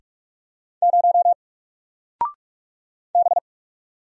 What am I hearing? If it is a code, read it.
Morse code, heard as a keyed tone, about 46 wpm, 700 Hz. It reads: 0 B
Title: Single Character (Letter:Number:Punctuation:Prosign) Alt 45wpm